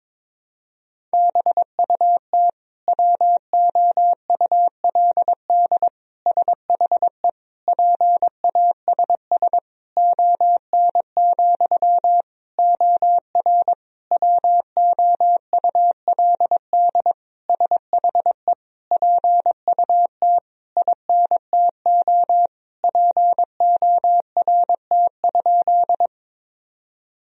Morse code, heard as a keyed tone, about 22 wpm, 700 Hz. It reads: BUT WOULD SHE PASS ON, OR WOULD SHE PUT INTO PORT?
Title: Sentences from Top 1000 Words - Review Part 1 - 22wpm